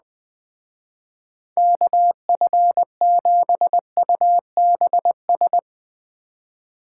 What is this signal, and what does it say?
Morse code, heard as a keyed tone, about 20 wpm, 700 Hz. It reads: KF7UBS